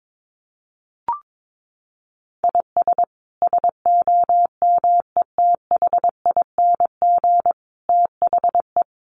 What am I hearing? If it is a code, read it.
Morse code, heard as a keyed tone, about 22 wpm, 700 Hz. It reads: IS SOMETHING THE
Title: Sentences from Top 700 Words 22wpm